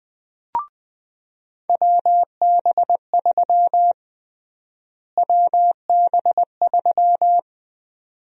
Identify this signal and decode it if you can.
Morse code, heard as a keyed tone, about 20 wpm, 700 Hz. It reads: WB3 WB3
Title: USA Call Sign Prefixes - Repeated 2x - 20wpm